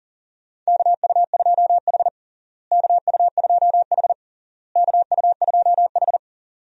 Morse code, heard as a keyed tone, about 40 wpm, 700 Hz. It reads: KU2H KU2H KU2H
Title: USA Call Signs - 4 Characters - Repeated 3x - 40wpm